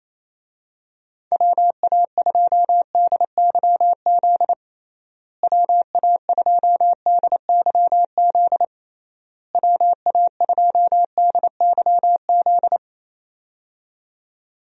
Morse code, heard as a keyed tone, about 28 wpm, 700 Hz. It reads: WA2DYZ WA2DYZ WA2DYZ
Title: USA Call Signs - 6 Characters - Repeated 3x - 28wpm